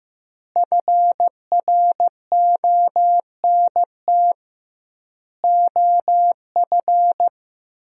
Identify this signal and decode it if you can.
Morse code, heard as a keyed tone, about 15 wpm, 700 Hz. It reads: FRONT OF
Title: 2x Word-Spacing - Sentences from Top 500 Words - 15wpm